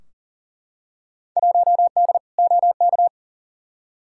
Morse code, heard as a keyed tone, about 40 wpm, 700 Hz. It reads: 1D OK
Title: Field Day Exchanges - 40wpm